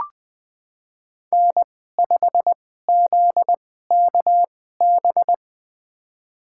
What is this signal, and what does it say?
Morse code, heard as a keyed tone, about 21 wpm, spaced as for 15 wpm, 700 Hz. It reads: N5ZKB
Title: USA Call Signs - 5 Characters - 2x Character Spacing - 20wpm